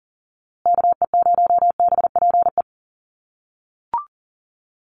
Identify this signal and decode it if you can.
Morse code, heard as a keyed tone, about 40 wpm, 700 Hz. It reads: KE0BPE
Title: USA Call Signs Repeated 3x 40wpm